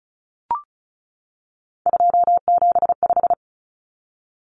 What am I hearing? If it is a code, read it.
Morse code, heard as a keyed tone, about 34 wpm, 700 Hz. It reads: 275
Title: Numbers - 3-Digit 35wpm